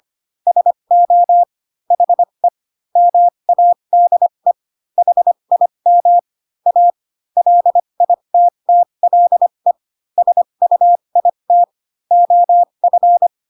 Morse code, heard as a keyed tone, about 25 wpm, spaced as for 22 wpm, 700 Hz. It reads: SO HE MADE HIM A LITTLE SUIT OF